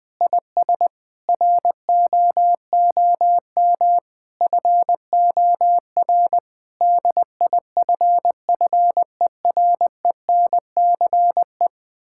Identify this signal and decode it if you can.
Morse code, heard as a keyed tone, about 19 wpm, 700 Hz. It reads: IS ROOM FOR DIFFERENCE
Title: Sentences from Top 1600 Words 20wpm